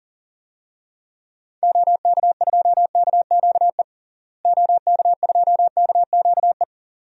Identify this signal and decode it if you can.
Morse code, heard as a keyed tone, about 40 wpm, 700 Hz. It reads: OK2KQE OK2KQE
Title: DXCC-100 Call Signs - 6 Characters - Repeated 2x - 40wpm